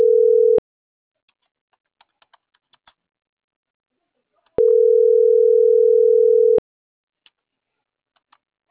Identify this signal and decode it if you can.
Ringback tone (North America), 440 + 480 Hz.